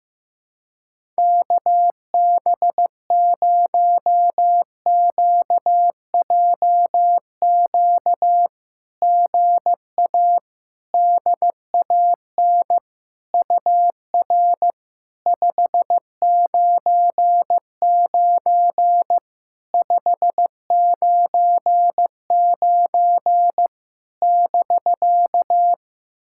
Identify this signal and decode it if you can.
Morse code, heard as a keyed tone, about 16 wpm, 700 Hz. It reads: KB0QJQ GA DAN UR 599 599 <BK>
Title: SOTA QSO Practice 15wpm